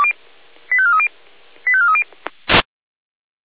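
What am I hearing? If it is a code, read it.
Five-tone selective call (ZVEI), heard as quick runs of five tones, tones of about 70 ms. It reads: 85430 85430
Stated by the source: Five-tone ZVEI selective call received on an SDR.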